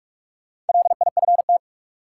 Morse code, heard as a keyed tone, about 47 wpm, 700 Hz. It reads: LIFT
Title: Sentences from Top 1000 Words - New Word Drill - 45wpm